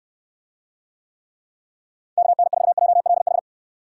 Morse code, heard as a keyed tone, about 64 wpm, 700 Hz. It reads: KN4JLH